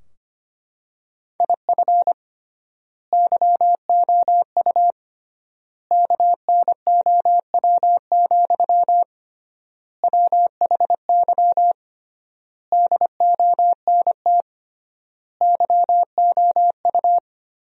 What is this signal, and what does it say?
Morse code, heard as a keyed tone, about 25 wpm, 700 Hz. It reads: IF YOU KNOW, WHY DONT YOU
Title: 3x Word-Spacing - Sentences from Top 300 Words - 25wpm